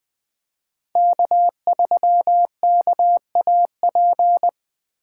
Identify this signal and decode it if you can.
Morse code, heard as a keyed tone, about 20 wpm, 700 Hz. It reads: K3KAP
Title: USA Call Signs - 5 Characters - 2x Character Spacing - 20wpm